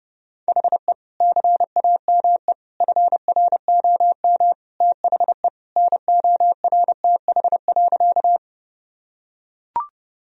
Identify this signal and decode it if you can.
Morse code, heard as a keyed tone, about 30 wpm, 700 Hz. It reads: HE CAME FROM THE NORTH.
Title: Sentences from Top 1000 Words - Review Part 1 - 30wpm